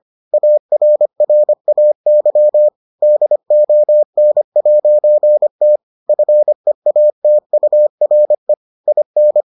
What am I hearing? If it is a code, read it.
Morse code, heard as a keyed tone, about 25 wpm, 600 Hz. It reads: ARRAY DON'T FEATURE IN